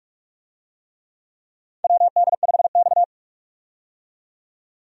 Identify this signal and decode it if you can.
Morse code, heard as a keyed tone, about 46 wpm, 700 Hz. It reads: WD5X
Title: USA Call Signs - 4 Characters - 45wpm